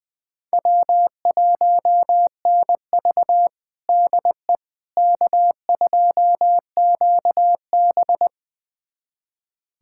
Morse code, heard as a keyed tone, about 21 wpm, 700 Hz. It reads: W1NV DE K2QB
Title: USA Callsign DE Callsign - 4 Characters - 20wpm